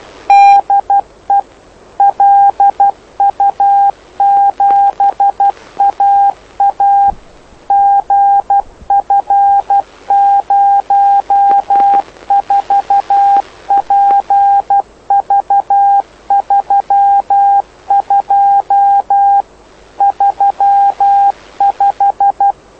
Morse code, heard as a keyed tone, about 12 wpm, 800 Hz. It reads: DE LU7AA GF04PV32 35
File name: lu4aao / lu7aa 09dic2011 exp planeador vuelo1